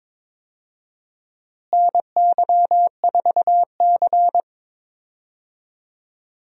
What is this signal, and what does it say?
Morse code, heard as a keyed tone, about 22 wpm, 700 Hz. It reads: NY4C
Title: USA Call Signs - 4 Characters - Repeated 3x - 22wpm